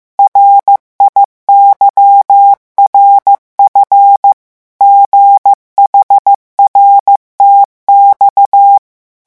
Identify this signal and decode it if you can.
Morse code, heard as a keyed tone, about 15 wpm, 800 Hz. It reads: RIYRF GHRTX
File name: morse mp3 / revision7-vitesse 14 mots